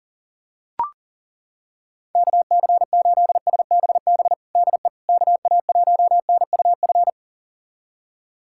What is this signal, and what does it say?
Morse code, heard as a keyed tone, about 37 wpm, 700 Hz. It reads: KC8SBB DE KA1NUF
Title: USA Callsign DE Callsign - 6 Characters - 40wpm